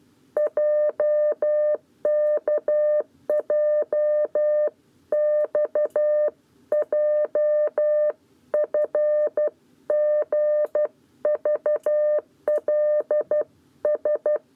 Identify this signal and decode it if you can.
Morse code, heard as a keyed tone, about 12 wpm, 600 Hz. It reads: JKJXJFGVLS
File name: 7b lezione cw 1